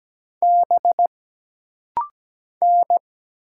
Morse code, heard as a keyed tone, about 17 wpm, 700 Hz. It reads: B N
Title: Single Letters - Rapid-Fire 17wpm